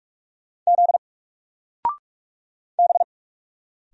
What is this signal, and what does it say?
Morse code, heard as a keyed tone, about 44 wpm, 700 Hz. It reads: Z B
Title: Single Letter-Number 45wpm